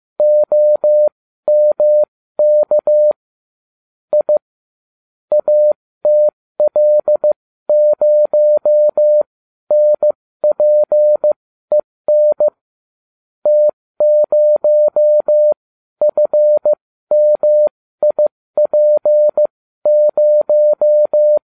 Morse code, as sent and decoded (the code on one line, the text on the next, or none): --- -- -.- / .. / .- - .-.. ----- -. .--. -. / - ----- ..-. -- .. .--. -----
OMK I ATL0NPEN T0FMIP0